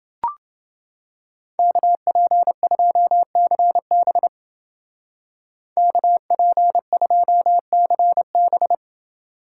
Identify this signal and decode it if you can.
Morse code, heard as a keyed tone, about 28 wpm, 700 Hz. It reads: KP2CB KP2CB